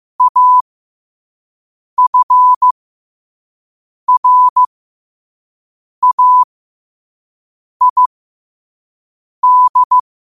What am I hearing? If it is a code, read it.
Morse code, heard as a keyed tone, about 15 wpm, spaced as for 5 wpm, 1000 Hz. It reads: AFRAID
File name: qotdfiles / 5wpm